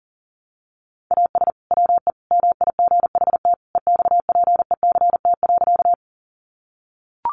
Morse code, heard as a keyed tone, about 37 wpm, 700 Hz. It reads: AS WE MIGHT EXPECT.
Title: Sentences from Top 1100 Words 40wpm